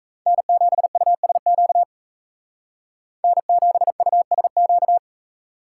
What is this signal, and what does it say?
Morse code, heard as a keyed tone, about 44 wpm, 700 Hz. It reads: N7USQ N7USQ